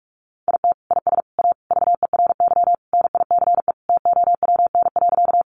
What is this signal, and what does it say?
Morse code, heard as a keyed tone, about 44 wpm, 700 Hz. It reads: IT IS A VERY NICE TOWN.
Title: Sentences from Top 1100 Words 45wpm